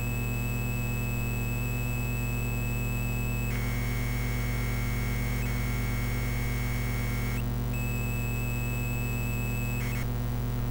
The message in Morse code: none